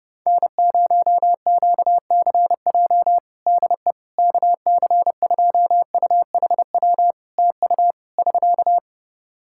Morse code, heard as a keyed tone, about 32 wpm, 700 Hz. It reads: N0QCJ DE KC2UHW TU <SK>
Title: K1USN SST QSO Practice 30wpm